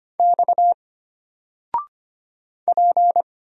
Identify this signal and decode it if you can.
Morse code, heard as a keyed tone, about 25 wpm, 700 Hz. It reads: X P
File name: Single-Letter-Number-25wpm